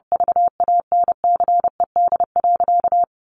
Morse code, heard as a keyed tone, about 30 wpm, 700 Hz. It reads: VANCED.
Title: Sentences from Top 1200 Words 30wpm